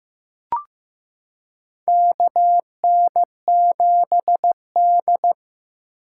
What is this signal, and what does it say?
Morse code, heard as a keyed tone, about 16 wpm, 700 Hz. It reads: KN7D
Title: K1USN SST QSO Practice - 15wpm